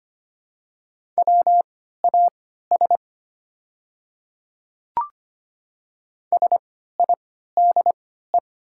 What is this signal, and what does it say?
Morse code, heard as a keyed tone, about 25 wpm, spaced as for 14 wpm, 700 Hz. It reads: WAS SIDE
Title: 3x Character Spacing - Top 100 Words - 25wpm